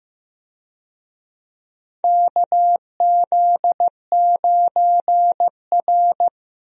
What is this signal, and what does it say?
Morse code, heard as a keyed tone, about 15 wpm, 700 Hz. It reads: KZ9R
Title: USA Call Signs - 4 Characters - Repeated 3x - 15wpm